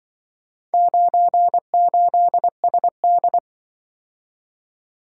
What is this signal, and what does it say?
Morse code, heard as a keyed tone, about 24 wpm, 700 Hz. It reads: 98SD
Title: Speed-Racing - Four Letter-Number Combinations - 20wpm